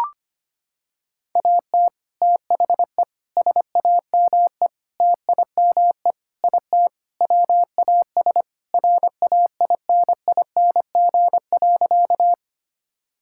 Morse code, heard as a keyed tone, about 25 wpm, 700 Hz. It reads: AT THE SAME TIME IT WAS RAINING.
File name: practice / 25wpm / Sentences with 3-Word Phrases - 25wpm